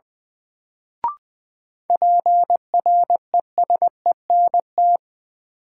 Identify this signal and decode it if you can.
Morse code, heard as a keyed tone, about 21 wpm, 700 Hz. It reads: PRESENT
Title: Sentences from Top 800 Words - New Word Drill - 20wpm